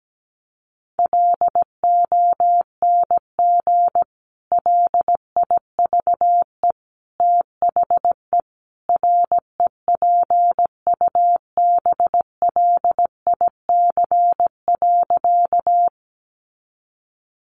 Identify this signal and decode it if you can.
Morse code, heard as a keyed tone, about 17 wpm, 700 Hz. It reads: LONG LIVE THE REPUBLIC.
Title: Sentences from Top 1500 Words 17wpm